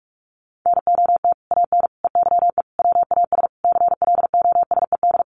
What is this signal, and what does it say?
Morse code, heard as a keyed tone, about 44 wpm, 700 Hz. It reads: NOT AN EYE WAS CLOSED
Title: Sentences from Top 1400 Words 45wpm